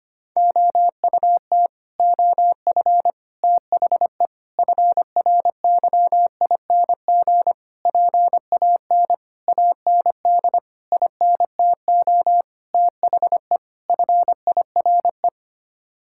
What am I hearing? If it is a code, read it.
Morse code, heard as a keyed tone, about 25 wpm, 700 Hz. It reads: OUT OF THE FRYING PAN AND INTO THE FIRE
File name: Sets of 9 Words - 25wpm